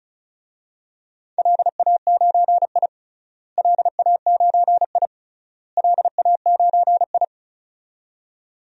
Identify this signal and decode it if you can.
Morse code, heard as a keyed tone, about 36 wpm, 700 Hz. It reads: LA9I LA9I LA9I
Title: DXCC-100 Call Signs - 4 Characters - Repeated 3x - 35wpm